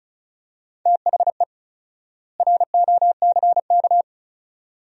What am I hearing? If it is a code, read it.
Morse code, heard as a keyed tone, about 34 wpm, 700 Hz. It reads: THE ROCK
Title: 4x Word-Spacing - Sentences from Top 400 Words - 35wpm